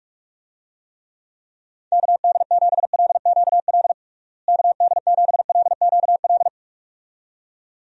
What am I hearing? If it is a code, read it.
Morse code, heard as a keyed tone, about 46 wpm, 700 Hz. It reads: KD7LQL KD7LQL
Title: USA Call Signs - 6 Characters - Repeated 2x - 45wpm